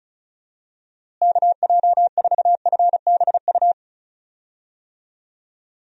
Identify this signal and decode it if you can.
Morse code, heard as a keyed tone, about 34 wpm, 700 Hz. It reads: KJ4FBU
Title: USA Call Signs - 6 Characters - Repeated 2x - 35wpm